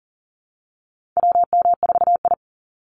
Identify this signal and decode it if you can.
Morse code, heard as a keyed tone, about 39 wpm, 700 Hz. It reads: WM4I
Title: Call Signs Repeated 3x - 40wpm